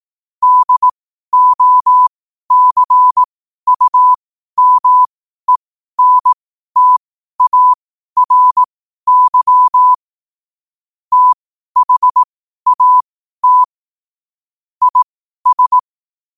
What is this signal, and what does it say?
Morse code, heard as a keyed tone, about 18 wpm, spaced as for 12 wpm, 1000 Hz. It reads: DOCUMENTARY THAT IS
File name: qotdfiles / 12wpm